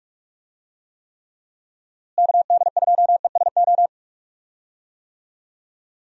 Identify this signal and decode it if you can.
Morse code, heard as a keyed tone, about 45 wpm, 700 Hz. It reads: KD2ESO
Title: USA Call Signs - 6 Characters - Repeated 3x - 45wpm